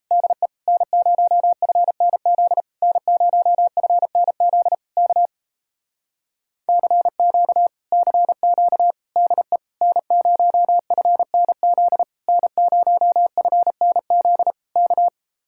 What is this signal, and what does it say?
Morse code, heard as a keyed tone, about 38 wpm, 700 Hz. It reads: DE N0FNZ N0FNZ K CQ CQ DE N0FNZ N0FNZ K